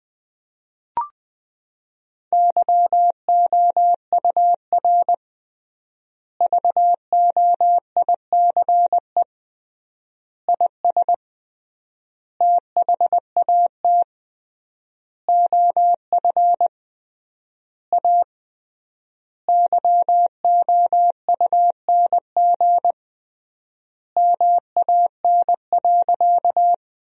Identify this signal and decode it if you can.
Morse code, heard as a keyed tone, about 20 wpm, 700 Hz. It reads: YOUR VOICE IS THAT OF A YOUNG MAN.
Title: 3x Word-Spacing - Sentences from Top 500 Words - 20wpm